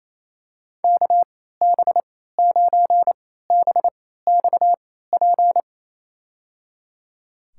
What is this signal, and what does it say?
Morse code, heard as a keyed tone, about 27 wpm, spaced as for 16 wpm, 700 Hz. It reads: KB9BXP
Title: USA Call Signs - 3x Character Spacing - 28wpm